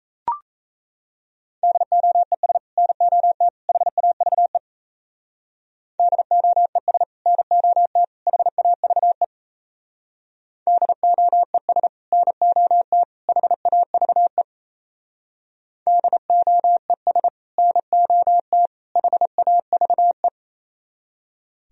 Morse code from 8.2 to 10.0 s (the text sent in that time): HAVE